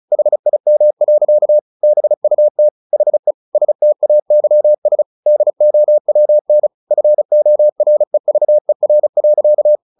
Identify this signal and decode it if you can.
Morse code, heard as a keyed tone, about 34 wpm, 600 Hz. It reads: HIM. BUT HE STAYS DOWN FOREVER.
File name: the old man and the sea / the old man and the sea 1 35wpm